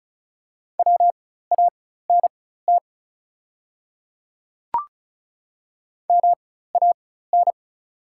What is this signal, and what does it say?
Morse code, heard as a keyed tone, about 36 wpm, spaced as for 16 wpm, 700 Hz. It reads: WANT MAN
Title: 4x Character Spacing - Top 200 Words - 35wpm